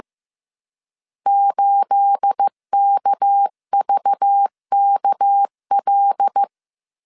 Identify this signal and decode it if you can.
Morse code, heard as a keyed tone, about 15 wpm, 800 Hz. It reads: 8KVKL
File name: CW / DET / 1 9 1500C S C 14wpm 3245